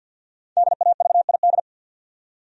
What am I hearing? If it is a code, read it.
Morse code, heard as a keyed tone, about 47 wpm, 700 Hz. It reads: DAVID
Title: USA Common Names - 50wpm